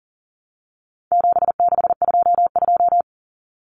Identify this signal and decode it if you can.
Morse code, heard as a keyed tone, about 40 wpm, 700 Hz. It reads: 7622
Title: Common Cut Numbers - 4-Digit 40wpm